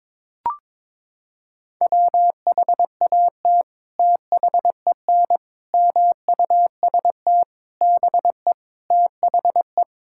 Morse code, heard as a keyed tone, about 22 wpm, 700 Hz. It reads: WHAT THEN MUST BE THE
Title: Sentences from Top 1000 Words - Review Part 5 - 22wpm